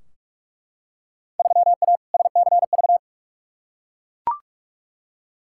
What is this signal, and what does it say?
Morse code, heard as a keyed tone, about 43 wpm, 700 Hz. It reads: 3A SCV
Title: Field Day Exchanges - 45wpm